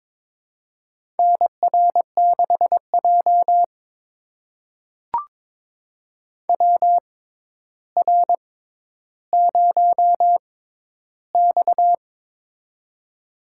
Morse code, heard as a keyed tone, about 22 wpm, spaced as for 11 wpm, 700 Hz. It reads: NR6J WR0X
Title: USA Call Signs - 4 Characters - 6x Character Spacing - 22wpm